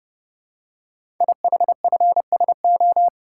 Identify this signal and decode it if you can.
Morse code, heard as a keyed tone, about 29 wpm, 700 Hz. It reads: IHFSO
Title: Code Groups - 5 Letters - 30wpm